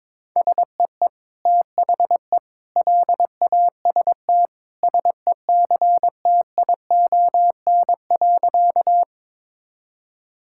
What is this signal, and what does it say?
Morse code, heard as a keyed tone, about 22 wpm, 700 Hz. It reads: SEE THE LAST SECTION.
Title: Sentences from Top 1000 Words - Review Part 5 - 22wpm